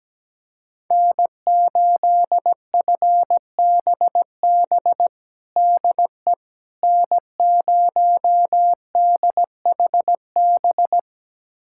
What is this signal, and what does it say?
Morse code, heard as a keyed tone, about 17 wpm, 700 Hz. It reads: N8FBB DE N0DHB
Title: USA Callsign DE Callsign - 5 Characters - 17wpm